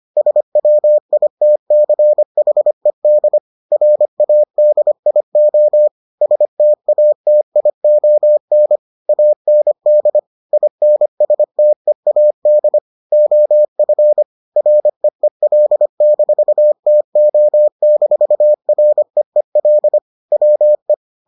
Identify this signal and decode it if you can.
Morse code, heard as a keyed tone, about 25 wpm, 600 Hz. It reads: SWITCHED RADIO STATION AND INSTEAD OF REEL-TO-REEL WE